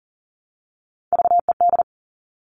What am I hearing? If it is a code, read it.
Morse code, heard as a keyed tone, about 40 wpm, 700 Hz. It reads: VED
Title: All Cut Numbers - 3-Digit 40wpm